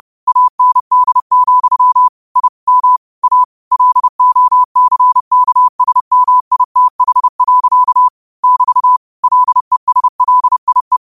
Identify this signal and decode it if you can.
Morse code, heard as a keyed tone, about 30 wpm, 1000 Hz. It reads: AND, I M A LOCKSMITH. <BT> LESLIE